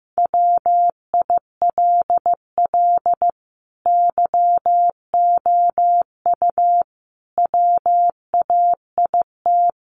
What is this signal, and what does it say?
Morse code, heard as a keyed tone, about 15 wpm, 700 Hz. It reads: WILL YOU WAIT
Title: Sentences from Top 900 Words 15wpm